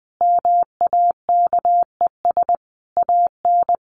Morse code, heard as a keyed tone, about 20 wpm, 700 Hz. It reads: MAKES AN
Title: A Sound Of Thunder 20wpm